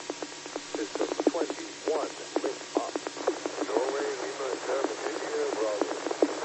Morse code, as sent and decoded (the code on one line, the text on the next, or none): none